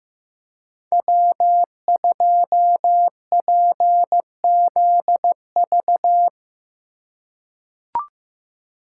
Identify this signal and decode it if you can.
Morse code, heard as a keyed tone, about 15 wpm, 700 Hz. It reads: W2PZV
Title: USA Call Signs - 5 Characters - 3x Character Spacing - 15wpm